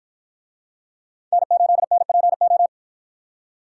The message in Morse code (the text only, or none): N8NPO